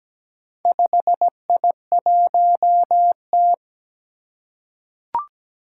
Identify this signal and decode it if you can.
Morse code, heard as a keyed tone, about 17 wpm, 700 Hz. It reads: 5I1T
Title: Four Challenging Letter-Number Combinations 17wpm